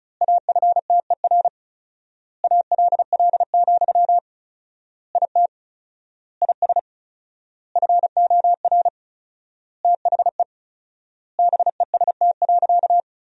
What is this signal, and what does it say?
Morse code, heard as a keyed tone, about 36 wpm, 700 Hz. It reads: AFTER ALL, IT IS FOR THE BEST.
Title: 4x Word-Spacing - Sentences from Top 500 Words - 35wpm